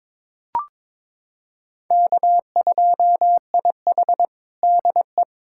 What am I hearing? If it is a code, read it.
Morse code, heard as a keyed tone, about 22 wpm, 700 Hz. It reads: K2IH DE
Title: USA Callsign DE Callsign - 4 Characters - 22wpm